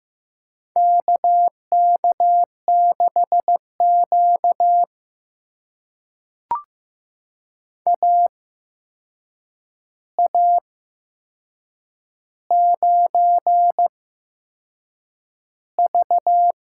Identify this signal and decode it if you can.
Morse code, heard as a keyed tone, about 15 wpm, spaced as for 6 wpm, 700 Hz. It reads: KK6Q AA9V
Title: USA Call Signs - 4 Characters - 8x Character Spacing - 15wpm